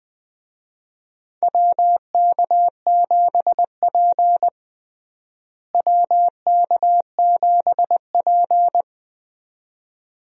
Morse code, heard as a keyed tone, about 20 wpm, 700 Hz. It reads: WK7P WK7P
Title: USA Call Signs - 4 Characters - Repeated 2x - 20wpm